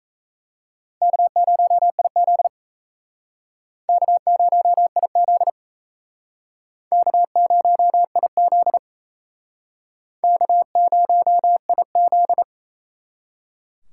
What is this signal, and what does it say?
Morse code, heard as a keyed tone, about 38 wpm, 700 Hz. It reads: K0IZ K0IZ K0IZ K0IZ